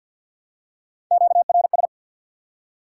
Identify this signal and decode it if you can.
Morse code, heard as a keyed tone, about 49 wpm, 700 Hz. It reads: QRS
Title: Top 90 QSO Elements 50wpm